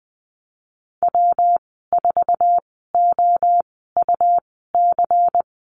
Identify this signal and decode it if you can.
Morse code, heard as a keyed tone, about 19 wpm, spaced as for 14 wpm, 700 Hz. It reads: W4OUC
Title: 128 Lesson 21g - Number 6 - Call Signs 20wpm